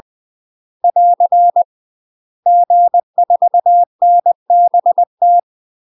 Morse code, heard as a keyed tone, about 20 wpm, 700 Hz. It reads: <AR> G4NBT
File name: UK MASTER QSOs TRAINING/20WPM